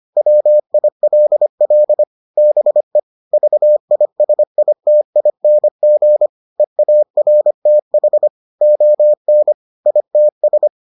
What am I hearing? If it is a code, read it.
Morse code, heard as a keyed tone, about 25 wpm, 600 Hz. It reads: WILL BE VISITING EARTH ON ITS